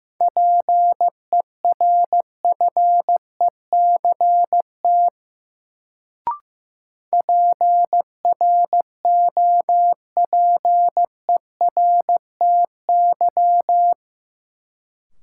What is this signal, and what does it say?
Morse code, heard as a keyed tone, about 15 wpm, 700 Hz. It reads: PERFECT PROPERTY
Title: Sentences from Top 1000 Words - New Word Drill - 15wpm